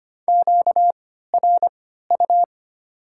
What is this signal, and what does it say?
Morse code, heard as a keyed tone, about 25 wpm, spaced as for 14 wpm, 690 Hz. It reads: QRU
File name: Top 90 QSO Elements - 3x Character Spacing - 25wpm